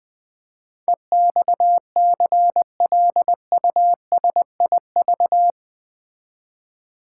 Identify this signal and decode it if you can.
Morse code, heard as a keyed tone, about 20 wpm, 700 Hz. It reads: EXCLUSIV
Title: Build A Word - 4 Letters At A Time - 11-Letter Words - 20wpm